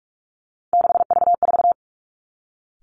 Morse code, heard as a keyed tone, about 46 wpm, 700 Hz. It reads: 6V4
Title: Three Challenging Letter-Number Combinations 45wpm